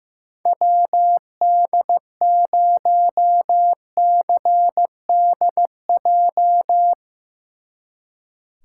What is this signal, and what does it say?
Morse code, heard as a keyed tone, about 15 wpm, 700 Hz. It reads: WD0CDJ
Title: Speed Racing - USA Call Signs 15wpm